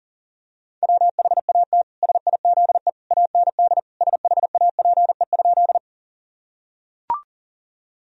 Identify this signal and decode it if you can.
Morse code, heard as a keyed tone, about 40 wpm, 700 Hz. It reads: WHAT SIZE AND SHAPE?
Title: Sentences from Top 1000 Words - Review Part 3 - 40wpm